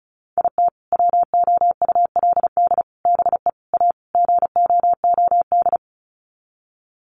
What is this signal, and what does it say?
Morse code, heard as a keyed tone, about 34 wpm, 700 Hz. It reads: IT WOULD BE A GOOD
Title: Sets of 5 Words Encore2 35wpm